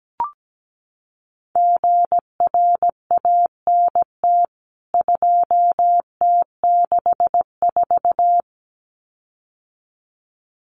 Morse code, heard as a keyed tone, about 17 wpm, 700 Hz. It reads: GRANT 2T64
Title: CWT Exchange 17wpm